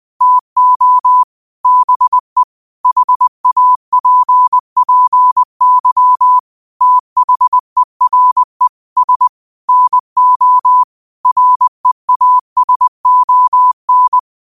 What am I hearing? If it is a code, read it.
Morse code, heard as a keyed tone, about 20 wpm, 1000 Hz. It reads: TO BE HAPPY THERE S NO REASON